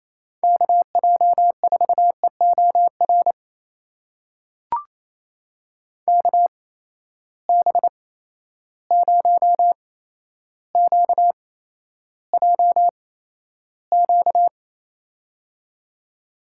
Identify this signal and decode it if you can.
Morse code, heard as a keyed tone, about 28 wpm, spaced as for 12 wpm, 700 Hz. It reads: KJ4EOR KB0QJQ
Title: USA Call Signs - 6 Characters - 8x Character Spacing - 28wpm